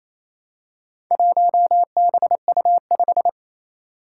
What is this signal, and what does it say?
Morse code, heard as a keyed tone, about 27 wpm, 700 Hz. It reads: 1BU5
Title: Four Challenging Letter-Number Combinations 28wpm